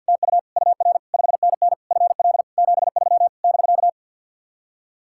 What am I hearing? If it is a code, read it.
Morse code, heard as a keyed tone, about 66 wpm, spaced as for 58 wpm, 700 Hz. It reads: TU UR 5NN FL 73 <BK>
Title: Speed-Racing Common QSO Phrases Encore - 50wpm